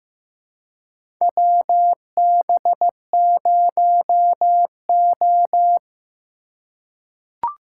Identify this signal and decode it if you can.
Morse code, heard as a keyed tone, about 15 wpm, 700 Hz. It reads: WB0O
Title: USA Call Signs - 4 Characters - 3x Character Spacing - 15wpm